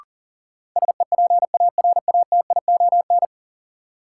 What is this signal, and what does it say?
Morse code, heard as a keyed tone, about 40 wpm, 700 Hz. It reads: SEPARATION
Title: Sets of 5 Words - New Word Drill - 40wpm